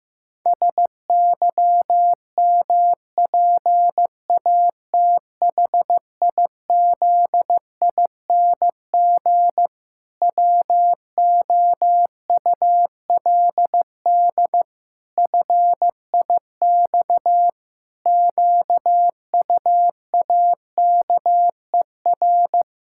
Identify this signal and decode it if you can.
Morse code, heard as a keyed tone, about 15 wpm, 700 Hz. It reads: SYMPATHIZING WOULD FIX QUAKER
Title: Pangram Challenge v2 - 15wpm